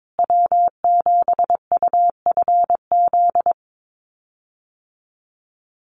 Morse code, heard as a keyed tone, about 22 wpm, 700 Hz. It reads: W7UFZ
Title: USA Call Signs Repeated 3x 22wpm